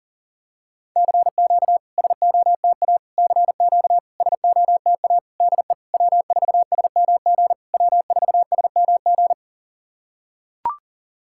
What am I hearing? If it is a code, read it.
Morse code, heard as a keyed tone, about 40 wpm, 700 Hz. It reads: CQ SOTA CQ SOTA DE W4SMG W4SMG
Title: SOTA QSO Practice 40wpm